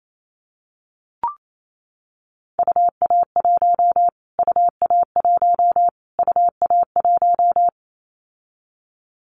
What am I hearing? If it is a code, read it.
Morse code, heard as a keyed tone, about 28 wpm, 700 Hz. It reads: UA1 UA1 UA1
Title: Callsign Prefixes for Top 25 DXCC - Repeated 3x - 28wpm